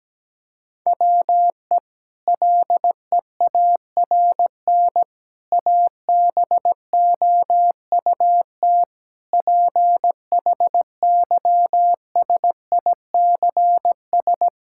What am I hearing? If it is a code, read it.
Morse code, heard as a keyed tone, about 17 wpm, 700 Hz. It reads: WE LEARN ABOUT PHYSICS
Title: Sentences from Top 1000 Words - Review Part 5 - 17wpm